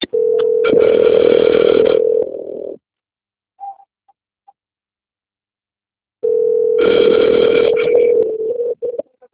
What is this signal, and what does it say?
Ringback tone (North America), 440 + 480 Hz.